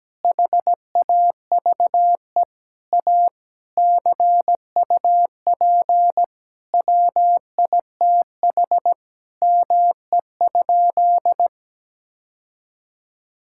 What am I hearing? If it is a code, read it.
Morse code, heard as a keyed tone, about 17 wpm, 700 Hz. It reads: HAVE A CUP WITH ME?
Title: Sentences from Top 1600 Words 17wpm